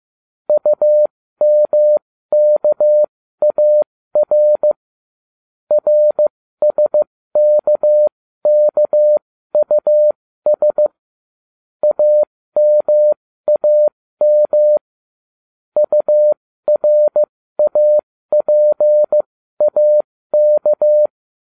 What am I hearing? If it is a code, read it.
Morse code, heard as a keyed tone, about 15 wpm, spaced as for 13 wpm, 600 Hz. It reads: UMKAR RSKKUS AMAM URAPAK